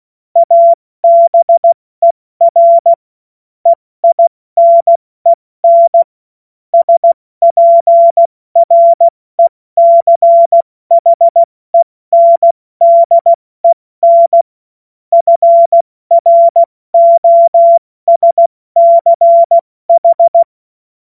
Morse code, heard as a keyed tone, about 16 wpm, 680 Hz. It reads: ABER EINEN SPRECHENDEN FROSCH
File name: Morsekurs / Witze / Witz 05. April 2022 70 F80